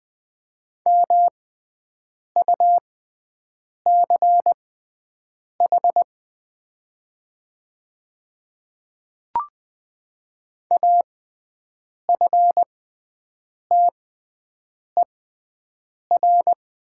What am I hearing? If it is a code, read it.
Morse code, heard as a keyed tone, about 20 wpm, spaced as for 7 wpm, 700 Hz. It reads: MUCH AFTER
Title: 6x Character Spacing - Top 300 Words - 20wpm